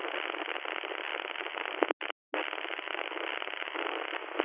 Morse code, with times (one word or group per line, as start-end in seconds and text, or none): none